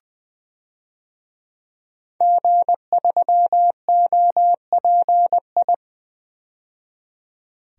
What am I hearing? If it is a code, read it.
Morse code, heard as a keyed tone, about 20 wpm, 700 Hz. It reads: G3OPI
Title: DXCC-100 Call Signs - 5 Characters - 20wpm